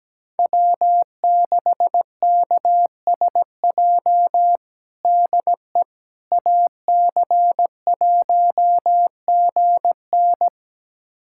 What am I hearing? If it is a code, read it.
Morse code, heard as a keyed tone, about 17 wpm, 700 Hz. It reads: W6KSJ DE AC1GN
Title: USA Callsign DE Callsign - 5 Characters - 17wpm